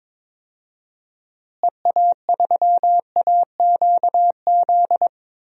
Morse code, heard as a keyed tone, about 22 wpm, 700 Hz. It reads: EA3AQZ
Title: DXCC-100 Call Signs - 6 Characters - Repeated 2x - 22wpm